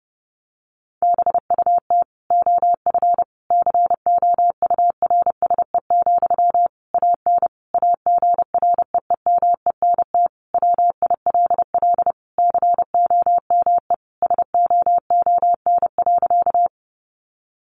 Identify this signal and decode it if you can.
Morse code, heard as a keyed tone, about 30 wpm, 700 Hz. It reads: BUT OF COURSE, AN AGREEMENT WILL COME SOON.
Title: Sentences from Top 1500 Words 30wpm